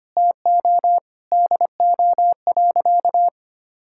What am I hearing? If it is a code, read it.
Morse code, heard as a keyed tone, about 25 wpm, 700 Hz. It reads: TO DO.
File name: practice / 25wpm / Sentences with 4-Word Phrases - 25wpm